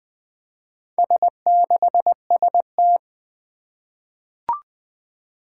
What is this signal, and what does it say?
Morse code, heard as a keyed tone, about 20 wpm, 700 Hz. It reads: S6ST
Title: Four Challenging Letter-Number Combinations 20wpm